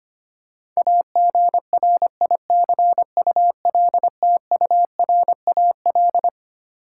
Morse code, heard as a keyed tone, about 25 wpm, 700 Hz. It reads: AGRICULTURAL
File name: Twelve Letter Words - 25wpm